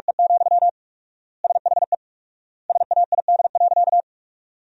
Morse code, heard as a keyed tone, about 46 wpm, 700 Hz. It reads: E, SHE SAID.
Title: 4x Word-Spacing - Sentences from Top 400 Words - 45wpm